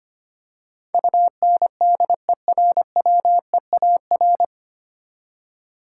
Morse code, heard as a keyed tone, about 25 wpm, 700 Hz. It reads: UNDERWEAR
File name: practice / 25wpm / 9 Letter Words - First 3 Letters Given - 25wpm